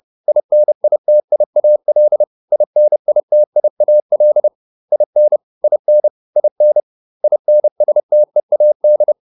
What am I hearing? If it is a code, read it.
Morse code, heard as a keyed tone, about 30 wpm, 600 Hz. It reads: INITIAL INITIAL IN IN IN INSTEAD